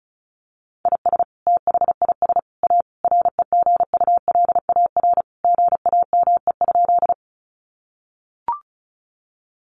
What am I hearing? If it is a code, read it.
Morse code, heard as a keyed tone, about 34 wpm, 700 Hz. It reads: IS THIS A REGULAR GAME?
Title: Sentences from Top 900 Words 35wpm